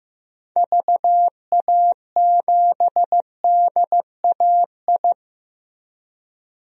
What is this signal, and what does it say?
Morse code, heard as a keyed tone, about 15 wpm, 700 Hz. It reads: VA7DAI
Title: DXCC-100 Call Signs - 6 Characters - Repeated 3x - 15wpm